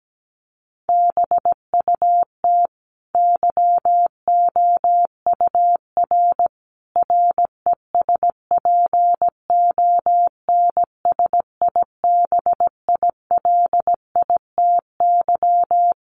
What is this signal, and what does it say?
Morse code, heard as a keyed tone, about 17 wpm, 700 Hz. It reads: BUT YOUR RESPONSIBILITY
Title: Sentences from Top 1200 Words 17wpm